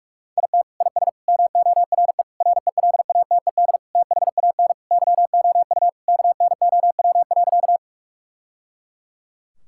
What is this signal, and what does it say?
Morse code, heard as a keyed tone, about 46 wpm, 700 Hz. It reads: IT IS MORE RELATED THAN YOU KNOW.
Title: Sentences from Top 1000 Words - Review Part 5 - 45wpm